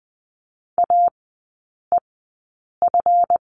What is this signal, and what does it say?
Morse code, heard as a keyed tone, about 20 wpm, spaced as for 8 wpm, 700 Hz. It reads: AEF
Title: 138 Lesson 23e - Letter f - Three Characters 20wpm